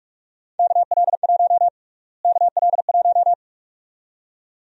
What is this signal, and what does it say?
Morse code, heard as a keyed tone, about 46 wpm, 700 Hz. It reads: KL1 KL1
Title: USA Call Sign Prefixes - Repeated 2x - 45wpm